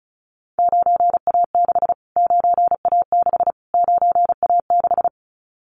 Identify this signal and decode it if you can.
Morse code, heard as a keyed tone, about 34 wpm, 700 Hz. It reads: 9A6 9A6 9A6
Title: Callsign Prefixes for Top 25 DXCC - Repeated 3x - 35wpm